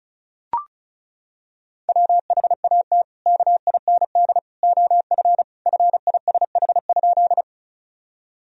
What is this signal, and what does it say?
Morse code, heard as a keyed tone, about 35 wpm, 700 Hz. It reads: WHAT KIND OF FISH?
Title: Sentences from Top 1000 Words - Review Part 1 - 35wpm